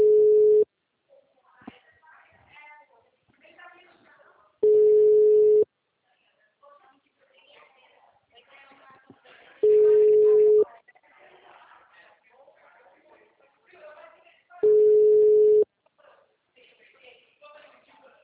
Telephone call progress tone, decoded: ringback tone

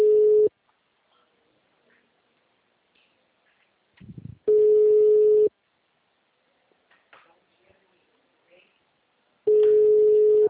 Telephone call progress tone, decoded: ringback tone

